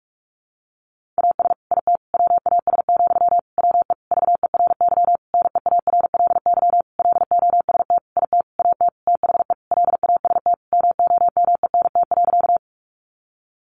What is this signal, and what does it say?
Morse code, heard as a keyed tone, about 44 wpm, 700 Hz. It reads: AS IT WAS, WE VERY NEARLY LOST IT AT THE LAST MOMENT.
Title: Sentences from Top 1400 Words 45wpm